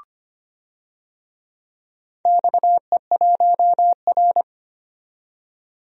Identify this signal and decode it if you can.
Morse code, heard as a keyed tone, about 25 wpm, 690 Hz. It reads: XE1R
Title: DXCC-100 Call Signs - 4 Characters - 25wpm